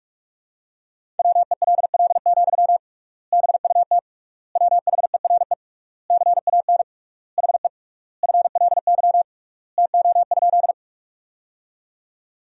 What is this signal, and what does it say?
Morse code, heard as a keyed tone, about 46 wpm, 700 Hz. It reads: WELL, BUT WHERE CAN HE FLY TO?
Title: 3x Word-Spacing - Sentences from Top 500 Words - 45wpm